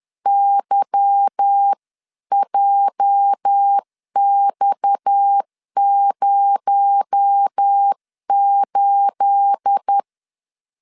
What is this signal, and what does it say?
Morse code, heard as a keyed tone, about 10 wpm, 800 Hz. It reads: YJX08
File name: LEZIONI / 9 28 10wpm 2842